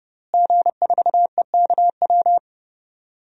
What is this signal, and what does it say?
Morse code, heard as a keyed tone, about 30 wpm, 700 Hz. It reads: G4EKW
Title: DXCC-100 Call Signs - 5 Characters - Repeated 3x - 30wpm